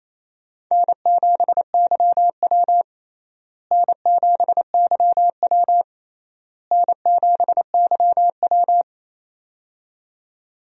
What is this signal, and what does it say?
Morse code, heard as a keyed tone, about 28 wpm, 700 Hz. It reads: N7YW N7YW N7YW
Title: USA Call Signs - 4 Characters - Repeated 3x - 28wpm